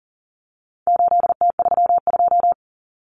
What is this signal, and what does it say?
Morse code, heard as a keyed tone, about 40 wpm, 700 Hz. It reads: 8T32
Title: Common Cut Numbers - 4-Digit 40wpm